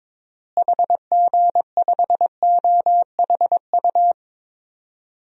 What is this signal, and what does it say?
Morse code, heard as a keyed tone, about 22 wpm, 700 Hz. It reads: HG5OHU
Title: DXCC-100 Call Signs - 6 Characters - Repeated 3x - 22wpm